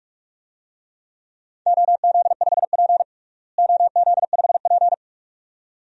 Morse code, heard as a keyed tone, about 47 wpm, 700 Hz. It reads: OZ5P OZ5P
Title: DXCC-100 Call Signs - 4 Characters - Repeated 2x - 45wpm